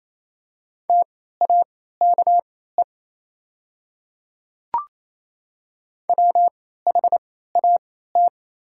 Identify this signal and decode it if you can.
Morse code, heard as a keyed tone, about 28 wpm, spaced as for 16 wpm, 700 Hz. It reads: TAKE WHAT
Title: 3x Character Spacing - Top 300 Words - 28wpm